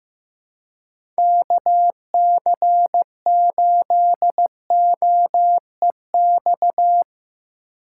Morse code, heard as a keyed tone, about 15 wpm, 700 Hz. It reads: KC8OEX
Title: USA Call Signs - 6 Characters - 4x Character Spacing - 15wpm